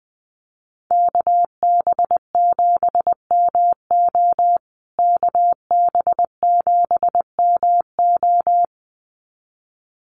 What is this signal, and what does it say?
Morse code, heard as a keyed tone, about 20 wpm, 700 Hz. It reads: KB7MO KB7MO
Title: USA Call Signs Repeated 2x 20wpm